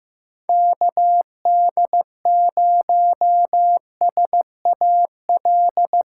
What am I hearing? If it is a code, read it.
Morse code, heard as a keyed tone, about 15 wpm, 700 Hz. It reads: KD0SAL
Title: USA Call Signs - 6 Characters - 3x Character Spacing - 15wpm